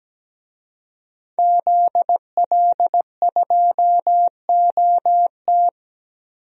Morse code, heard as a keyed tone, about 17 wpm, 700 Hz. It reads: ZL2OT
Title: DXCC-100 Call Signs - 5 Characters - 17wpm